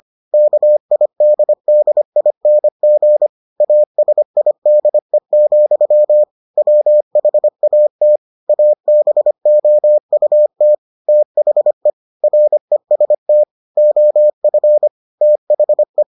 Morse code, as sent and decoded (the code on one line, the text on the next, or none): -.- .. -.. -.. .. -. --. / .- ... .. -.. --..-- / .-- .... .- - / .- -... --- ..- - / - .... / .-. ... - / --- ..-. / - .... .
KIDDING ASIDE, WHAT ABOUT THE REST OF THE